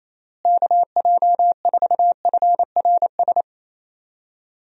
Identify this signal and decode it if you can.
Morse code, heard as a keyed tone, about 28 wpm, 700 Hz. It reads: KJ4FRS
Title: USA Call Signs - 6 Characters - Repeated 3x - 28wpm